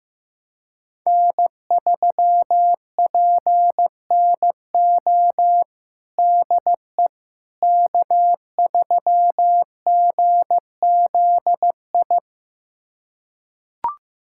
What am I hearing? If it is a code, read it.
Morse code, heard as a keyed tone, about 15 wpm, 700 Hz. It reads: N3PNO DE K3GZI
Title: USA Callsign DE Callsign - 5 Characters - 15wpm